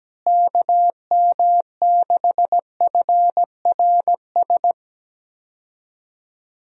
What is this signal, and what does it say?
Morse code, heard as a keyed tone, about 17 wpm, 700 Hz. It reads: KM6FRS
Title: USA Call Signs - 6 Characters - Repeated 3x - 17wpm